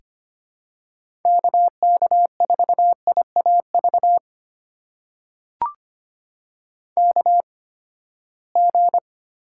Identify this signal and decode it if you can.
Morse code, heard as a keyed tone, about 25 wpm, spaced as for 17 wpm, 700 Hz. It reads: KK4IAV KG
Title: USA Call Signs - 6 Characters - 8x Character Spacing - 25wpm